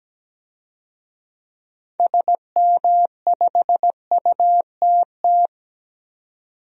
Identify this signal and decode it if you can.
Morse code, heard as a keyed tone, about 17 wpm, 700 Hz. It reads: SM5UTT
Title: DXCC-100 Call Signs - 6 Characters - 17wpm